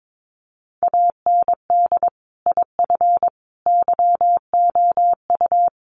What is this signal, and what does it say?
Morse code, heard as a keyed tone, about 22 wpm, 700 Hz. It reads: AND IF YOU
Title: Sets of 5 Words Encore 22wpm